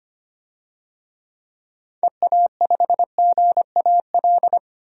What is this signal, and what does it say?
Morse code, heard as a keyed tone, about 25 wpm, 700 Hz. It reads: EA5GAL
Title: DXCC-100 Call Signs - 6 Characters - 25wpm